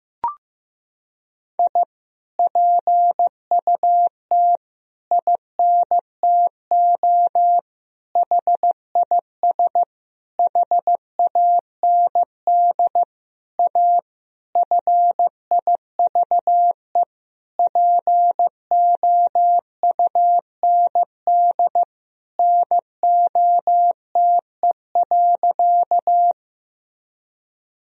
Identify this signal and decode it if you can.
Morse code, heard as a keyed tone, about 15 wpm, 700 Hz. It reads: I PUT INTO HIS HAND A FIVE POUND NOTE.
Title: Sentences from Top 1000 Words - Review Part 2 - 15wpm